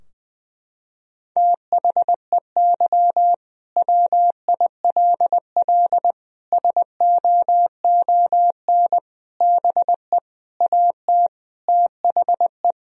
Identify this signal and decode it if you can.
Morse code, heard as a keyed tone, about 21 wpm, 700 Hz. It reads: THEY WILL SOON BE AT THE
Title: Sentences from Top 1000 Words - Review Part 4 - 20wpm